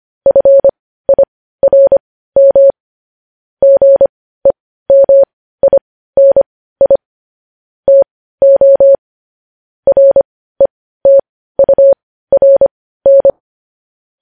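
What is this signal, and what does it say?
Morse code, heard as a keyed tone, about 25 wpm, spaced as for 15 wpm, 550 Hz. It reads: FIRM GEMINI TO RETURN
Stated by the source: Morse news headlines sent at 15 words per minute.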